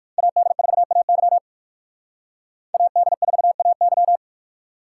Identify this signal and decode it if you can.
Morse code, heard as a keyed tone, about 57 wpm, 700 Hz. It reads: AD4AY AD4AY